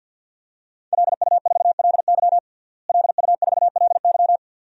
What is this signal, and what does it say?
Morse code, heard as a keyed tone, about 46 wpm, 700 Hz. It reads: LU4LY LU4LY
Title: DXCC-100 Call Signs - 5 Characters - Repeated 3x - 50wpm